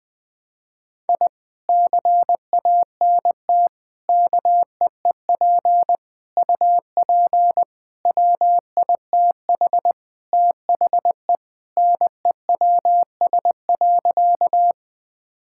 Morse code, heard as a keyed tone, about 19 wpm, 700 Hz. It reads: I CANT KEEP UP WITH THE NEWS.
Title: Sentences with 2-Word Phrases - 20wpm